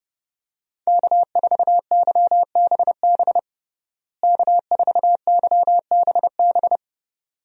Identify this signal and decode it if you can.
Morse code, heard as a keyed tone, about 30 wpm, 700 Hz. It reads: K4YBB K4YBB
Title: USA Call Signs - 5 Characters - Repeated 2x - 30wpm